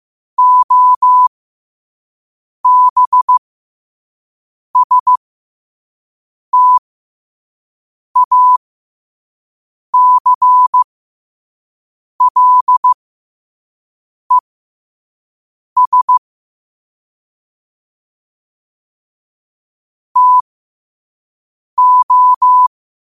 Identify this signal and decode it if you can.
Morse code, heard as a keyed tone, about 15 wpm, spaced as for 5 wpm, 1000 Hz. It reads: OBSTACLES TO
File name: qotdfiles / 5wpm